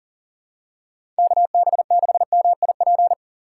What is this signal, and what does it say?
Morse code, heard as a keyed tone, about 40 wpm, 700 Hz. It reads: KB6MIP
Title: USA Call Signs - 2x Character Spacing - 40wpm